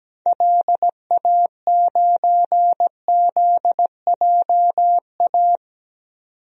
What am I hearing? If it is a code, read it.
Morse code, heard as a keyed tone, about 17 wpm, 700 Hz. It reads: LA9ZJA
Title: DXCC-100 Call Signs - 6 Characters - 17wpm